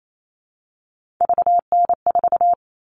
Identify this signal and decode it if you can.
Morse code, heard as a keyed tone, about 28 wpm, 700 Hz. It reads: VN4
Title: All Cut Numbers - 3-Digit 28wpm